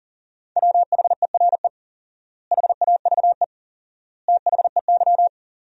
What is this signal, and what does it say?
Morse code, heard as a keyed tone, about 38 wpm, 700 Hz. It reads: WHERE HAVE THEY
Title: 4x Word-Spacing - Sentences from Top 200 Words - 40wpm